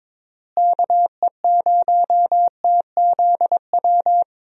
Morse code, heard as a keyed tone, about 22 wpm, 700 Hz. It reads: KE0TZW
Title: USA Call Signs - 6 Characters - Repeated 3x - 22wpm